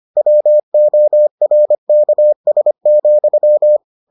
Morse code, heard as a keyed tone, about 25 wpm, 600 Hz. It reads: WORKS,